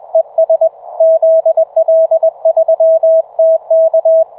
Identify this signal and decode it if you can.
Morse code, heard as a keyed tone, about 24 wpm, 660 Hz. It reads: ES ZL3TK